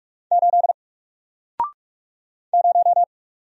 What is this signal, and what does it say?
Morse code, heard as a keyed tone, about 46 wpm, 700 Hz. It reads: 8 0
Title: Single Character (Letter:Number:Punctuation:Prosign) 45wpm